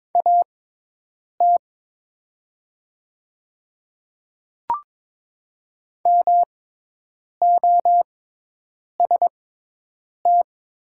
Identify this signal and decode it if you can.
Morse code, heard as a keyed tone, about 22 wpm, spaced as for 8 wpm, 700 Hz. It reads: AT MOST
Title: 6x Character Spacing - Top 300 Words - 22wpm